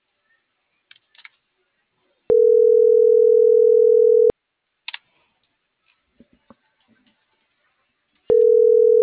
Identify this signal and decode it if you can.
Ringback tone (North America), 440 + 480 Hz.